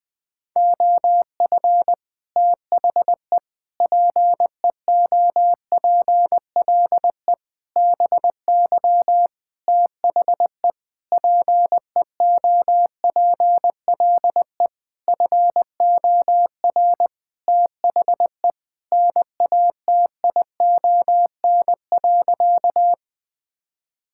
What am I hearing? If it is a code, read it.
Morse code, heard as a keyed tone, about 20 wpm, 700 Hz. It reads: OF THE PEOPLE BY THE PEOPLE FOR THE NATION.
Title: Sentences with 8-Word Phrases - 20wpm